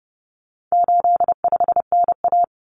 Morse code, heard as a keyed tone, about 30 wpm, 700 Hz. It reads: 85NA
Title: Common Cut Numbers - 4-Digit 30wpm